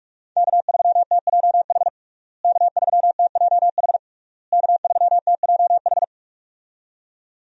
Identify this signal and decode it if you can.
Morse code, heard as a keyed tone, about 46 wpm, 700 Hz. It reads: K3TJH K3TJH K3TJH
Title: USA Call Signs - 5 Characters - Repeated 3x - 45wpm